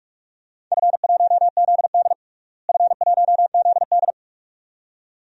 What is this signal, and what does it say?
Morse code, heard as a keyed tone, about 41 wpm, 700 Hz. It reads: F1ZD F1ZD
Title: DXCC-100 Call Signs - 4 Characters - Repeated 2x - 45wpm